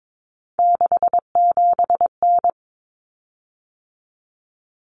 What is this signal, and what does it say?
Morse code, heard as a keyed tone, about 22 wpm, 700 Hz. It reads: 67N
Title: Common Cut Numbers - 3-Digit 22wpm